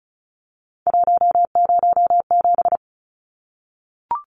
Morse code, heard as a keyed tone, about 34 wpm, 700 Hz. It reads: 107
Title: Numbers - 3-Digit 35wpm